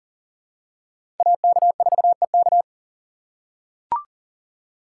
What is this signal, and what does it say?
Morse code, heard as a keyed tone, about 40 wpm, 700 Hz. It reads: AK4EK
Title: USA Call Signs - 5 Characters - 3x Character Spacing - 40wpm